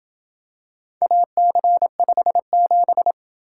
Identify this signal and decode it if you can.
Morse code, heard as a keyed tone, about 28 wpm, 700 Hz. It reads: AC57